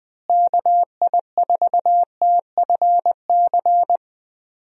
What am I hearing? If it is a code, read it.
Morse code, heard as a keyed tone, about 21 wpm, 700 Hz. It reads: KI4TFC
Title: USA Call Signs - 6 Characters - Repeated 3x - 20wpm